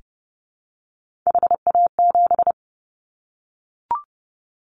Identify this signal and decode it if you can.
Morse code, heard as a keyed tone, about 30 wpm, 700 Hz. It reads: HA7
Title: Callsign Prefixes for Top 25 DXCC - 30wpm